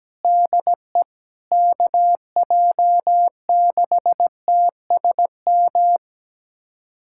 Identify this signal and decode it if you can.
Morse code, heard as a keyed tone, about 17 wpm, 700 Hz. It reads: DE KJ6TSM
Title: USA Callsign DE Callsign - 6 Characters - 17wpm